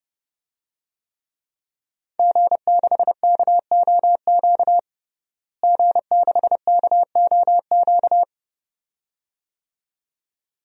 Morse code, heard as a keyed tone, about 32 wpm, 700 Hz. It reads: G6KOQ G6KOQ
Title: DXCC-100 Call Signs - 5 Characters - Repeated 2x - 30wpm